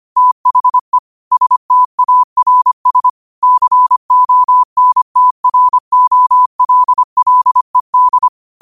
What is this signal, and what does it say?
Morse code, heard as a keyed tone, about 25 wpm, 1000 Hz. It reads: THE STARS CONTROLLED
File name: qotdfiles / 25wpm